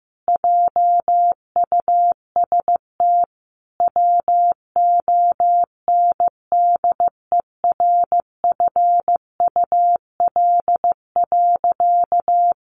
Morse code, heard as a keyed tone, about 15 wpm, 700 Hz. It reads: JUST WONDERFUL.
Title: Sentences from Top 1100 Words 15wpm